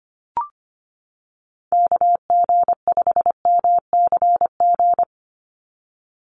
Morse code, heard as a keyed tone, about 25 wpm, 700 Hz. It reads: KG5MCG
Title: USA Call Signs 25wpm